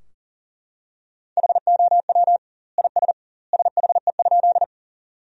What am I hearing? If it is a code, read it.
Morse code, heard as a keyed tone, about 40 wpm, 700 Hz. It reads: HOW IS SHE?
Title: 2x Word-Spacing - Sentences from Top 100 Words - 40wpm